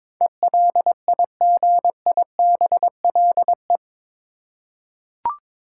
Morse code, heard as a keyed tone, about 21 wpm, 700 Hz. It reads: ELIGIBLE
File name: practice / 22wpm / Eight Letter Words - 22wpm